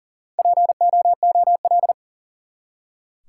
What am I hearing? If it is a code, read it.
Morse code, heard as a keyed tone, about 37 wpm, 700 Hz. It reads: POOL
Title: Four Letter Words - 40wpm